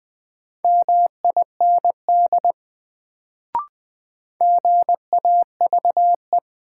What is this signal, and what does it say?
Morse code, heard as a keyed tone, about 20 wpm, 700 Hz. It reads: MIND GAVE
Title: Sentences from Top 500 Words - New Word Drill - 20wpm